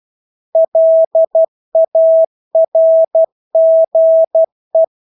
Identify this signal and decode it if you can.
Morse code, heard as a keyed tone, about 12 wpm, 650 Hz. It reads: LARGE
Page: Engelske ord 12 wpm